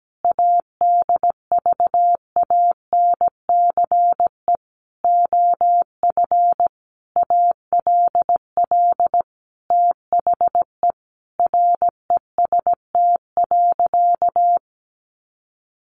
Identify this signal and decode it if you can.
Morse code, heard as a keyed tone, about 17 wpm, 700 Hz. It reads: ADVANCE OF ALL THE REST.
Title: Sentences from Top 1400 Words 17wpm